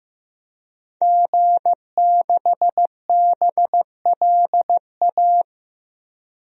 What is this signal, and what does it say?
Morse code, heard as a keyed tone, about 15 wpm, 700 Hz. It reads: G6BLA
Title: DXCC-100 Call Signs - 5 Characters - Repeated 2x - 15wpm